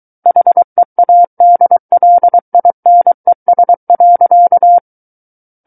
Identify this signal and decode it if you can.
Morse code, heard as a keyed tone, about 23 wpm, 700 Hz. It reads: HEADLINES.
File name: Book One / 14-Chapter 14 - 23 WPM